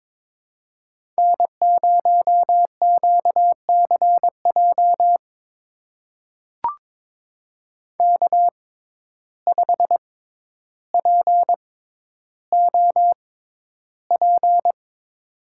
Morse code, heard as a keyed tone, about 21 wpm, spaced as for 11 wpm, 700 Hz. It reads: N0QCJ K5POP
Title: USA Call Signs - 6x Character Spacing - 22wpm